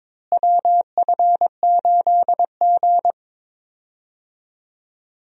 Morse code, heard as a keyed tone, about 22 wpm, 700 Hz. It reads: WF8G
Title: USA Call Signs - 4 Characters - Repeated 3x - 22wpm